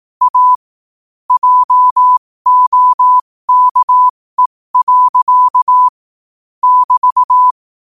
Morse code, heard as a keyed tone, about 18 wpm, spaced as for 15 wpm, 1000 Hz. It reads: A JOKE. <BT>
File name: qotdfiles / 15wpm